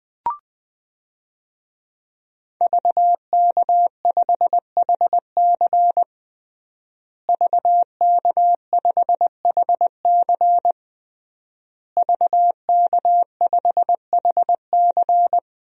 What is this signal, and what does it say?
Morse code, heard as a keyed tone, about 21 wpm, 700 Hz. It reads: VK5HC VK5HC VK5HC
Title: DXCC-100 Call Signs - 5 Characters - Repeated 3x - 20wpm